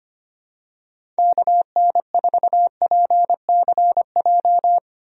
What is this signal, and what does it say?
Morse code, heard as a keyed tone, about 25 wpm, 700 Hz. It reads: KN4PCJ
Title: USA Call Signs - 6 Characters - Repeated 3x - 25wpm